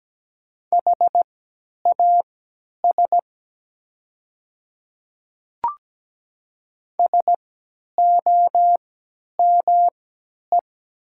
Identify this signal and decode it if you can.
Morse code, heard as a keyed tone, about 17 wpm, spaced as for 10 wpm, 700 Hz. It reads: HAS SOME
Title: 3x Character Spacing - Top 200 Words - 17wpm